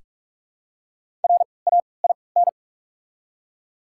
Morse code, heard as a keyed tone, about 45 wpm, 690 Hz. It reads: R A I N
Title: 3x Character Spacing - Top 500 Words - 45wpm